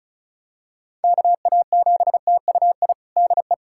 Morse code, heard as a keyed tone, about 34 wpm, 700 Hz. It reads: KA7TUI DE
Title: USA Callsign DE Callsign 35wpm